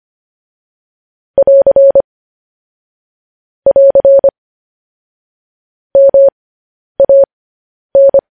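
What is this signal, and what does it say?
Morse code, heard as a keyed tone, about 25 wpm, spaced as for 10 wpm, 550 Hz. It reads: <AR> <AR> MAN
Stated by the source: Morse code sending of news headlines, speed 10.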